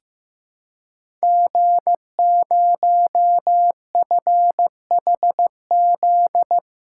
Morse code, heard as a keyed tone, about 15 wpm, 700 Hz. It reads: G0FHZ
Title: Code Groups - 5 Letter:Numbers - 15wpm